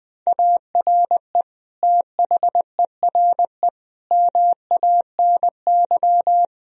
Morse code, heard as a keyed tone, about 21 wpm, 700 Hz. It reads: ARE THERE MANY
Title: Sentences from Top 1000 Words - Review Part 1 - 20wpm